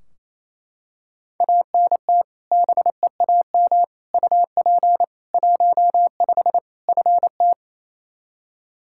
Morse code, heard as a keyed tone, about 28 wpm, 700 Hz. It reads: ANT BEAM UP 15 FT
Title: Practice QSO - Exchange 2 - Info - 28wpm